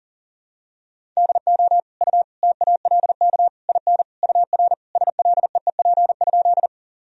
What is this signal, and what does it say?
Morse code, heard as a keyed tone, about 40 wpm, 700 Hz. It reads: DO U TALK IN UR SLEEP?
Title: Sentences from Top 500 Words and Abbreviations 40wpm